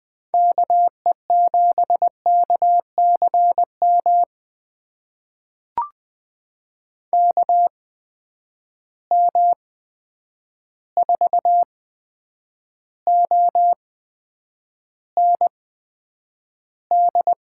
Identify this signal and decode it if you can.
Morse code, heard as a keyed tone, about 20 wpm, spaced as for 9 wpm, 700 Hz. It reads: KE7KCM KM4OND
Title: USA Call Signs - 6 Characters - 8x Character Spacing - 20wpm